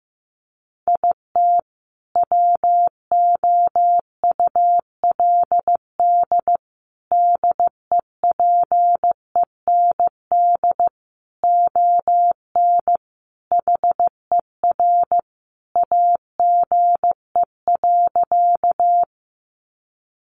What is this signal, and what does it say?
Morse code, heard as a keyed tone, about 15 wpm, 700 Hz. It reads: IT WOULD DEPEND ON HER AGE.
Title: Sentences from Top 1300 Words 15wpm